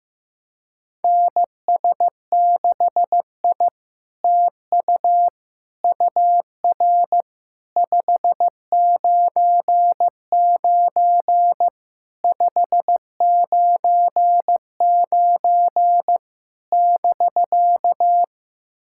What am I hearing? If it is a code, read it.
Morse code, heard as a keyed tone, about 15 wpm, 700 Hz. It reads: NS6I TU UR 599 599 <BK>
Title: SOTA QSO Practice 15wpm